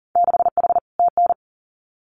Morse code, heard as a keyed tone, about 43 wpm, 700 Hz. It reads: 6H TN
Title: Winter Field Day Exchanges Encore 40wpm